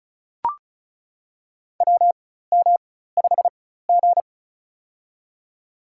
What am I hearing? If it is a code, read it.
Morse code, heard as a keyed tone, about 35 wpm, spaced as for 16 wpm, 700 Hz. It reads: WM5G
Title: USA Call Signs - 4 Characters - 4x Character Spacing - 35wpm